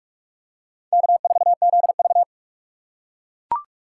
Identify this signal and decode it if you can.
Morse code, heard as a keyed tone, about 44 wpm, 700 Hz. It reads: K4ZV